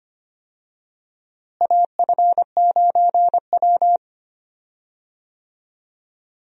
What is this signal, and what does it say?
Morse code, heard as a keyed tone, about 25 wpm, 700 Hz. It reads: AF9W
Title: USA Call Signs - 4 Characters - Repeated 2x - 25wpm